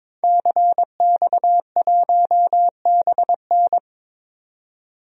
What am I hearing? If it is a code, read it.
Morse code, heard as a keyed tone, about 22 wpm, 700 Hz. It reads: CX1BN
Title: DXCC-100 Call Signs - 5 Characters - 22wpm